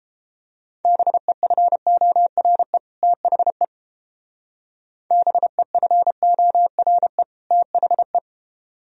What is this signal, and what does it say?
Morse code, heard as a keyed tone, about 29 wpm, 700 Hz. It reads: BEFORE THE BEFORE THE